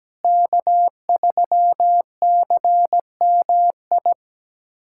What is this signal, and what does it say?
Morse code, heard as a keyed tone, about 17 wpm, 700 Hz. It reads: K3CMI
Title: USA Callsign DE Callsign 17wpm